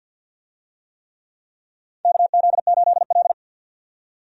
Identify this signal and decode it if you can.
Morse code, heard as a keyed tone, about 52 wpm, 700 Hz. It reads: KZ8L
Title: USA Call Signs - 4 Characters - 50wpm